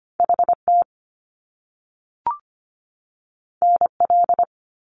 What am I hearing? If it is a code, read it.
Morse code, heard as a keyed tone, about 25 wpm, 700 Hz. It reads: HT NL
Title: Top Two Letter Combinations in Words 25wpm